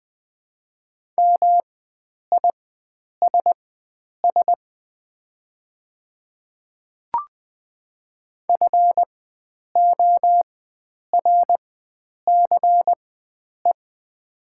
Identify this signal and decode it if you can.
Morse code, heard as a keyed tone, about 20 wpm, spaced as for 9 wpm, 700 Hz. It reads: MISS FORCE
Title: 4x Character Spacing - Top 500 Words - 20wpm